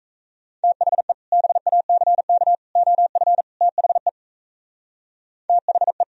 Morse code, heard as a keyed tone, about 43 wpm, 700 Hz. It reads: THE BACK OF THE THE